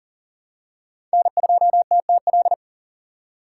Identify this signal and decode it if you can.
Morse code, heard as a keyed tone, about 40 wpm, 700 Hz. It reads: N2TTL
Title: USA Call Signs - 5 Characters - 4x Character Spacing - 40wpm